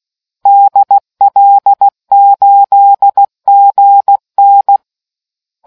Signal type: Morse code